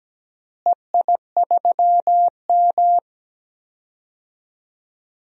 Morse code, heard as a keyed tone, about 17 wpm, 700 Hz. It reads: EI3M
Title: DXCC-100 Call Signs - 4 Characters - Repeated 3x - 17wpm